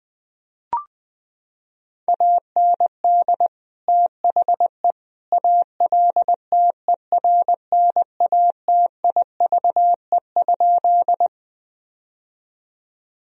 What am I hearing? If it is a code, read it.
Morse code, heard as a keyed tone, about 20 wpm, 700 Hz. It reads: AND THE ALTERNATIVE?
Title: Sentences from Top 1000 Words - Review Part 4 - 20wpm